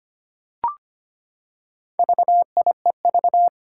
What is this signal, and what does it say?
Morse code, heard as a keyed tone, about 25 wpm, 700 Hz. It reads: VIEV
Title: Four Challenging Letter-Number Combinations 25wpm